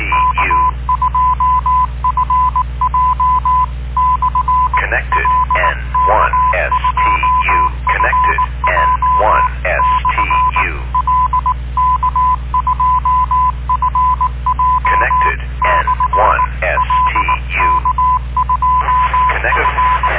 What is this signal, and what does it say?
Morse code, heard as a keyed tone, about 19 wpm, 1010 Hz. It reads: K2FJ/L K2FJ/L K2FJ/L K2F